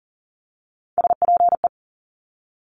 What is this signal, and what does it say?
Morse code, heard as a keyed tone, about 42 wpm, 700 Hz. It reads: SPE
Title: Top Three Letter Combinations in Words 40wpm